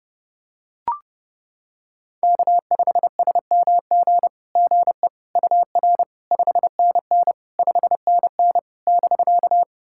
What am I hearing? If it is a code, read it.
Morse code, heard as a keyed tone, about 30 wpm, 700 Hz. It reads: K5SMG GE UR 5NN 5NN <BK>
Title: SOTA QSO Practice 30wpm